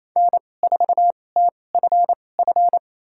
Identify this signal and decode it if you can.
Morse code, heard as a keyed tone, about 27 wpm, spaced as for 20 wpm, 700 Hz. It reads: N4TFF
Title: USA Call Signs - 5 Characters - 2x Character Spacing - 28wpm